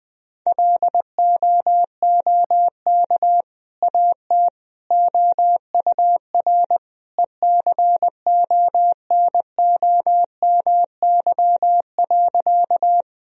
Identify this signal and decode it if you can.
Morse code, heard as a keyed tone, about 21 wpm, 700 Hz. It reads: LOOK AT OUR ECONOMY.
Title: Sentences from Top 1000 Words - Review Part 3 - 20wpm